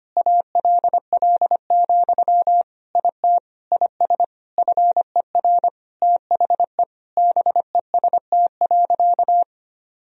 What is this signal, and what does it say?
Morse code, heard as a keyed tone, about 25 wpm, 700 Hz. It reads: ALL, IT IS FER THE BEST.
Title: Sentences from Top 500 Words and Abbreviations 25wpm